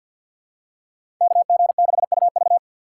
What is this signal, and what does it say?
Morse code, heard as a keyed tone, about 53 wpm, 700 Hz. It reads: KG6UV
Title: USA Call Signs - 5 Characters - 4x Character Spacing - 50wpm